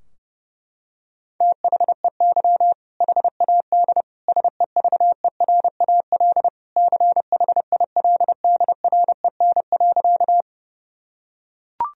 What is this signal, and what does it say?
Morse code, heard as a keyed tone, about 30 wpm, 700 Hz. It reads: THEY HAD SEVERAL CHILDREN.
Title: Sentences from Top 1000 Words - Review Part 2 - 30wpm